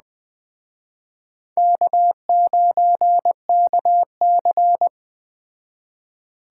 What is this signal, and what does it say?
Morse code, heard as a keyed tone, about 21 wpm, 700 Hz. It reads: K9KC